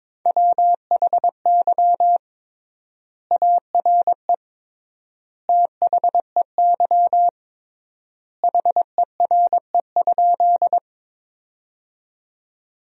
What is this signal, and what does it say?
Morse code, heard as a keyed tone, about 22 wpm, 700 Hz. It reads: WHY ARE THEY HERE?
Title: 3x Word-Spacing - Sentences from Top 200 Words - 22wpm